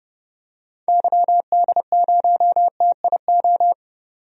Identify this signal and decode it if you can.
Morse code, heard as a keyed tone, about 31 wpm, 700 Hz. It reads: YD0TIO
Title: DXCC-100 Call Signs - 6 Characters - 30wpm